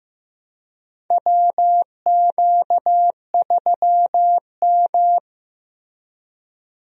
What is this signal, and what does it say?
Morse code, heard as a keyed tone, about 15 wpm, 700 Hz. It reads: WQ3M
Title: USA Call Signs - 4 Characters - 15wpm